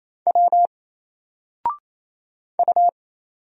Morse code, heard as a keyed tone, about 27 wpm, 700 Hz. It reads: W U
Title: Single Character (Letter:Number:Punctuation:Prosign) 28wpm